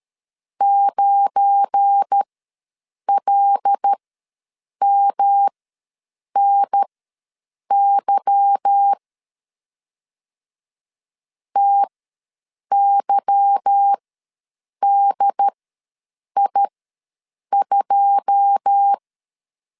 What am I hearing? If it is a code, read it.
Morse code, heard as a keyed tone, about 13 wpm, spaced as for 7 wpm, 800 Hz. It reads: 9LMNY TYDI2
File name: LEZIONI / 9 2 ws 12wpm 3406